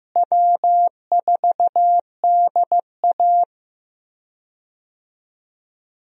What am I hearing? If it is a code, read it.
Morse code, heard as a keyed tone, about 15 wpm, 700 Hz. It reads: W4DA
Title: USA Call Signs - 4 Characters - Repeated 2x - 15wpm